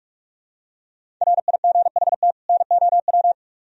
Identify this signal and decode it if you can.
Morse code, heard as a keyed tone, about 46 wpm, 700 Hz. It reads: RIGHT NOW